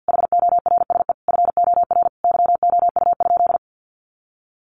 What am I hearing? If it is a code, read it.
Morse code, heard as a keyed tone, about 50 wpm, 700 Hz. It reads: HORSE FOR YOU?
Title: Sentences from Top 1400 Words 50wpm